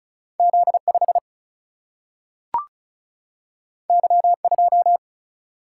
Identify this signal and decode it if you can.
Morse code, heard as a keyed tone, about 36 wpm, 700 Hz. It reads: Z5 Y2
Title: Single-Letter Single-Number 35wpm